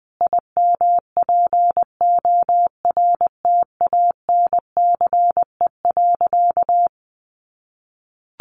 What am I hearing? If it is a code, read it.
Morse code, heard as a keyed tone, about 20 wpm, 700 Hz. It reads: IMPORTANCE.
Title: Sentences from Top 1200 Words 20wpm